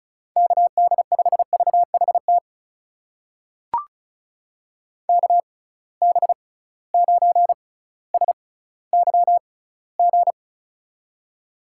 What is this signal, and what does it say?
Morse code, heard as a keyed tone, about 36 wpm, spaced as for 18 wpm, 700 Hz. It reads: KD5VHT KB9SYG
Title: USA Call Signs - 6 Characters - 6x Character Spacing - 35wpm